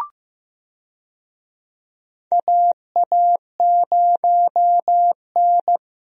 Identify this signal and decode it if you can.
Morse code, heard as a keyed tone, about 15 wpm, 700 Hz. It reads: AA0N
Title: USA Call Signs - 4 Characters - Repeated 2x - 15wpm